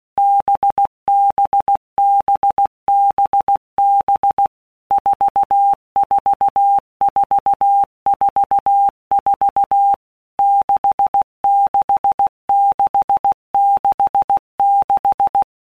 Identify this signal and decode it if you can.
Morse code, heard as a keyed tone, about 16 wpm, 800 Hz. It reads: BBBBB 44444 66666
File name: Basic Letter Groups By WPM's / Groups@15 / VB46@15